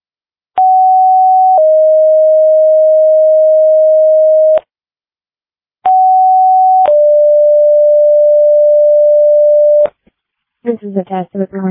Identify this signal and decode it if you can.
Two-tone page, heard as a one-second tone, then a three-second tone, about 750 then 620 Hz, about 750 then 590 Hz, with speech. It is heard from 0.6 to 9.9 s.